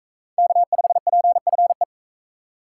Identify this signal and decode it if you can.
Morse code, heard as a keyed tone, about 41 wpm, 700 Hz. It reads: K5PFE